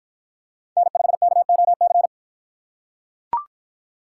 Morse code, heard as a keyed tone, about 44 wpm, 700 Hz. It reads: N5KOC